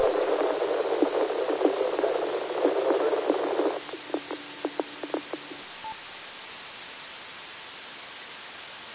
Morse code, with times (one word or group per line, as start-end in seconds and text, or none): none